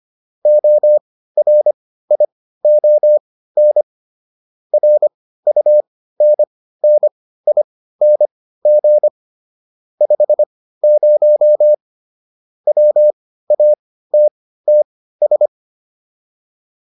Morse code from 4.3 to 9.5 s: RUNNING

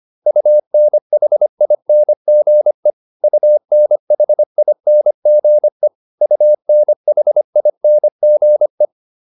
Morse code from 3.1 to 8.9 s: UNHINGE UNHINGE

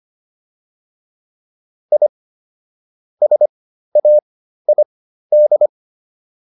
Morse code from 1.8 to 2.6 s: I

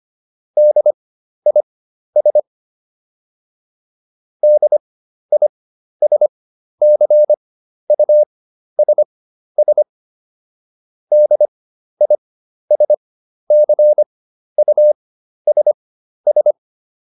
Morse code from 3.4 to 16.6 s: DISCUSS DISCUSS

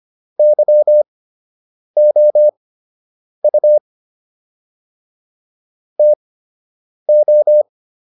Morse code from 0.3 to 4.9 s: YOU

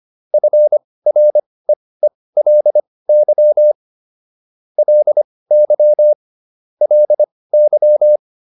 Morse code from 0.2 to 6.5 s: FREELY LY

